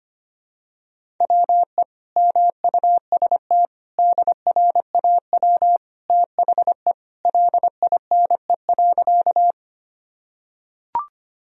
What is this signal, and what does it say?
Morse code, heard as a keyed tone, about 25 wpm, 700 Hz. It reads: WE MUST DRAW THE LINE.
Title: Sentences from Top 1000 Words - Review Part 1 - 25wpm